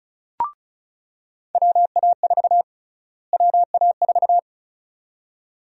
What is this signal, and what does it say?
Morse code, heard as a keyed tone, about 34 wpm, 700 Hz. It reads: WA4 WA4
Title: USA Call Sign Prefixes - Repeated 2x - 35wpm